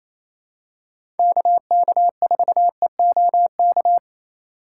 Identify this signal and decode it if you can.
Morse code, heard as a keyed tone, about 28 wpm, 700 Hz. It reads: KK4EOK
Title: USA Call Signs - 3x Character Spacing - 28wpm